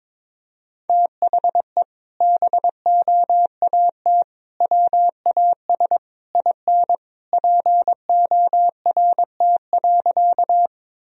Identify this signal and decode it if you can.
Morse code, heard as a keyed tone, about 22 wpm, 700 Hz. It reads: THE BOAT WAS IN PORT.
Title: Sentences from Top 1000 Words - Review Part 1 - 22wpm